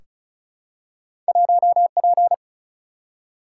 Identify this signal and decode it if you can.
Morse code, heard as a keyed tone, about 36 wpm, 700 Hz. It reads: 1P
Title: Single-Letter Single-Number 35wpm